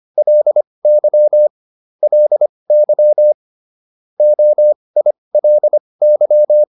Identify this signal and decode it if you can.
Morse code, heard as a keyed tone, about 25 wpm, spaced as for 20 wpm, 600 Hz. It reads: LY LY OILY